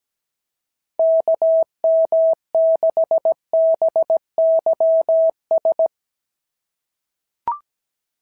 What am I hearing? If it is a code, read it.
Morse code, heard as a keyed tone, about 16 wpm, 650 Hz. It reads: KM6BYS
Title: SOTA QSO Practice 17wpm